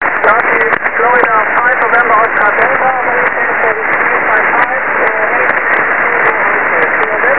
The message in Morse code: none